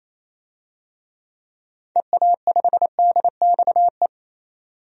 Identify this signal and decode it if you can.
Morse code, heard as a keyed tone, about 28 wpm, 700 Hz. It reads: EA5DXE
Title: DXCC-100 Call Signs - 6 Characters - 28wpm